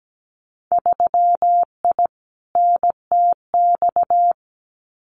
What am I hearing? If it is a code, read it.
Morse code, heard as a keyed tone, about 17 wpm, 700 Hz. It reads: 3I NTX
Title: Winter Field Day Exchanges 17wpm